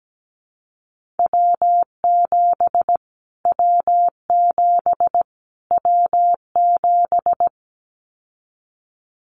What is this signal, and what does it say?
Morse code, heard as a keyed tone, about 17 wpm, 700 Hz. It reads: W7 W7 W7
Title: Callsign Prefixes for Top 25 DXCC - Repeated 3x - 17wpm